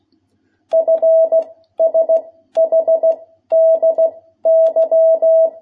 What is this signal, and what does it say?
Morse code, heard as a keyed tone, about 16 wpm, 650 Hz. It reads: F S H D Y
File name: jh8cbh / 2021cw